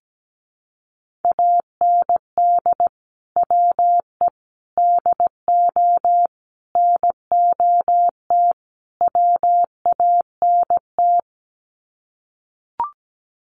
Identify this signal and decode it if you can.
Morse code, heard as a keyed tone, about 17 wpm, 700 Hz. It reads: AND WE DO NOT WANT
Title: Sets of 5 Words Encore2 17wpm